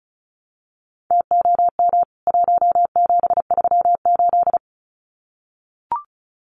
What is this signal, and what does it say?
Morse code, heard as a keyed tone, about 34 wpm, 700 Hz. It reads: TOM 1738
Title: CWT Exchange 35wpm